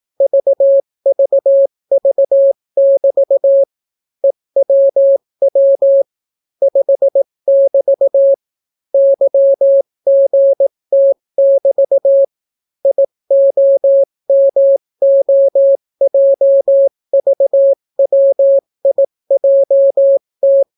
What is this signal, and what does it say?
Morse code, heard as a keyed tone, about 18 wpm, 550 Hz. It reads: VVV<BT> EWW 5<BT> YGT<BT> IOMOJVWIJT